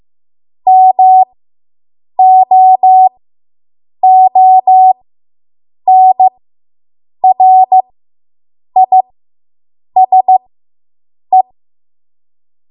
Morse code, heard as a keyed tone, about 15 wpm, spaced as for 7 wpm, 750 Hz. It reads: MOONRISE